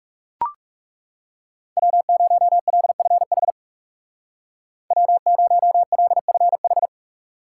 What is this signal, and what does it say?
Morse code, heard as a keyed tone, about 35 wpm, 700 Hz. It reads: W0LFH W0LFH
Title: Speed Racing - USA Call Signs 30wpm